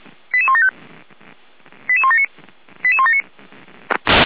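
Five-tone selective call (ZVEI), heard as quick runs of five tones, tones of about 70 ms. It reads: 88175 88179 88179
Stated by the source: ZVEI five-tone selective call code 88171, received by SDR.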